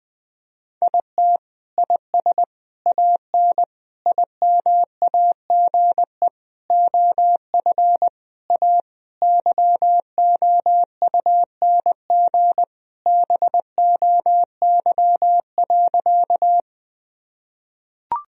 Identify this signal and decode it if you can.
Morse code, heard as a keyed tone, about 20 wpm, 700 Hz. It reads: IT IS AN IMAGE OF A YOUNG BOY.
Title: Sentences from Top 1000 Words - Review Part 4 - 20wpm